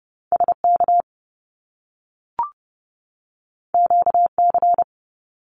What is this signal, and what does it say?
Morse code, heard as a keyed tone, about 30 wpm, 700 Hz. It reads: SK QC
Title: Canadian Provinces and Territories 30wpm